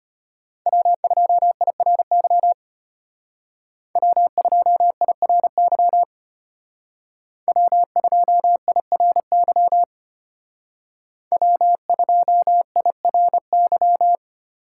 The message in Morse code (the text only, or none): W2IRY W2IRY W2IRY W2IRY